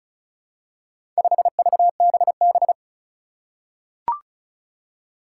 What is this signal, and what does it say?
Morse code, heard as a keyed tone, about 36 wpm, 700 Hz. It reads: 5VBB
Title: Four Challenging Letter-Number Combinations 35wpm